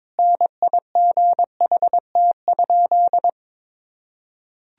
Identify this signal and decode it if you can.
Morse code, heard as a keyed tone, about 22 wpm, 700 Hz. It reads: NIGHT?
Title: Sentences from Top 1000 Words - Review Part 2 - 22wpm